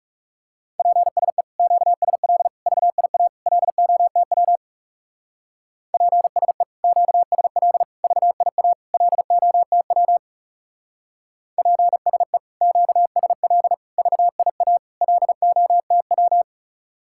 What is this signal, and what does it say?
Morse code, heard as a keyed tone, about 36 wpm, 700 Hz. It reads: PSE QSL VIA LOTW PSE QSL VIA LOTW PSE QSL VIA LOTW
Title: Speed-Racing Common QSO Phrases Encore - 30wpm